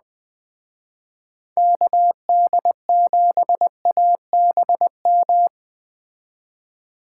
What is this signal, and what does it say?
Morse code, heard as a keyed tone, about 20 wpm, 700 Hz. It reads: KD7ABM